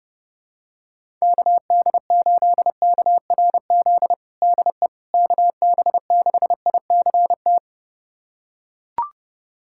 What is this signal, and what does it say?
Morse code, heard as a keyed tone, about 31 wpm, 700 Hz. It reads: KD8KRZ DE KB6ICT
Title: USA Callsign DE Callsign 30wpm